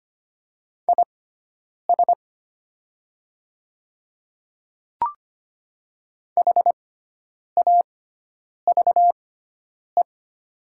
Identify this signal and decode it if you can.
Morse code, heard as a keyed tone, about 25 wpm, spaced as for 9 wpm, 700 Hz. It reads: IS HAVE
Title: 6x Character Spacing - Top 100 Words - 25wpm